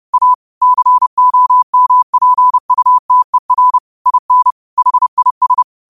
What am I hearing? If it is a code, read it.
Morse code, heard as a keyed tone, about 30 wpm, 1000 Hz. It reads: A COMPUTER IN HIS
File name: qotdfiles / 30wpm